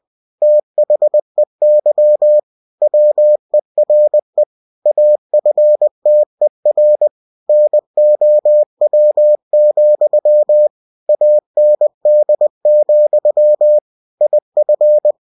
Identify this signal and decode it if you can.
Morse code, heard as a keyed tone, about 20 wpm, 600 Hz. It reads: THEY WERE AFTER NOW, AND, IF
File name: cw-books / winnie-the-pooh / winnie-20wpm-page-0022